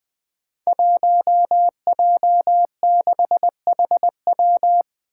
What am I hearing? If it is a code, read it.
Morse code, heard as a keyed tone, about 20 wpm, 700 Hz. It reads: 1J6HW
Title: Five Challenging Letter-Number Combinations 20wpm